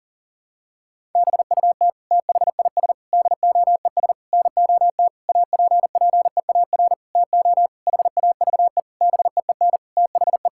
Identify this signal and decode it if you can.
Morse code, heard as a keyed tone, about 40 wpm, 700 Hz. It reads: BUT THIS DOES NOT APPEAR TO HAVE BEEN THE
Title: Sentences from Top 1000 Words - Review Part 6 - 40wpm